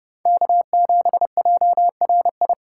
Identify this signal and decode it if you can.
Morse code, heard as a keyed tone, about 28 wpm, 700 Hz. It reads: K7JRI